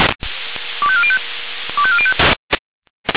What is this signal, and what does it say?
Five-tone selective call (ZVEI), heard as quick runs of five tones, tones of about 70 ms. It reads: 25405 25405
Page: ZVEI